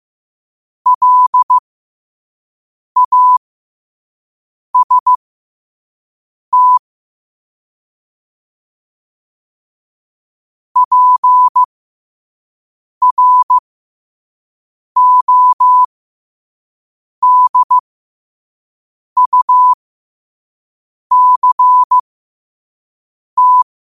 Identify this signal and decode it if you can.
Morse code, heard as a keyed tone, about 15 wpm, spaced as for 5 wpm, 1000 Hz. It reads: LAST PRODUCT
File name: qotdfiles / 5wpm